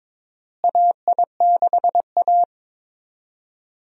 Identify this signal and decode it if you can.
Morse code, heard as a keyed tone, about 22 wpm, 700 Hz. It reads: AI6A
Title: USA Call Signs - 4 Characters - Repeated 2x - 22wpm